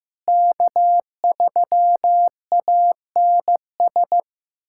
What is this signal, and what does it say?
Morse code, heard as a keyed tone, about 15 wpm, 700 Hz. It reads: K3ANS
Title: USA Callsign DE Callsign 15wpm